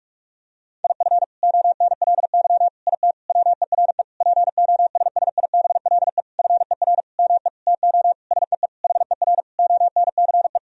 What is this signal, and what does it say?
Morse code, heard as a keyed tone, about 48 wpm, 700 Hz. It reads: IF ONLY IT WERE POSSIBLE FER ME TO SEE HER ONCE
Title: Sentences from Top 500 Words and Abbreviations 45wpm